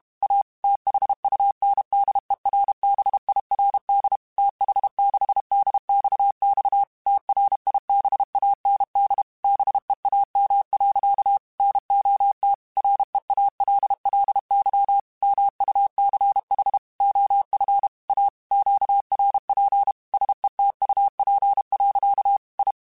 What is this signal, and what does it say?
Morse code, heard as a keyed tone, about 32 wpm, 800 Hz. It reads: A THUNDERBIRD TH6DXX TRIBAND BEAM. NOT REALLY MUCH OF A QRP SETUP. I